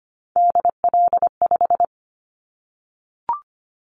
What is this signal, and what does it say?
Morse code, heard as a keyed tone, about 25 wpm, 700 Hz. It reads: DL5
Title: Callsign Prefixes for Top 25 DXCC - 25wpm